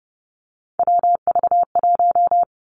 Morse code, heard as a keyed tone, about 30 wpm, 700 Hz. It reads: WV1
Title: Three Challenging Letter-Number Combinations 30wpm